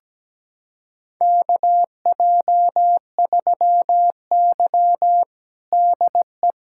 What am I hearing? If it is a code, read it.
Morse code, heard as a keyed tone, about 17 wpm, 700 Hz. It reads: KJ3Y DE
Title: USA Callsign DE Callsign - 4 Characters - 17wpm